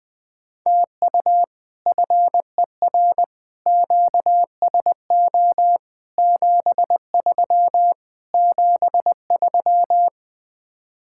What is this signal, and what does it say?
Morse code, heard as a keyed tone, about 20 wpm, 700 Hz. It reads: TU FER QSO 73 73
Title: Speed-Racing Common QSO Phrases Encore - 20wpm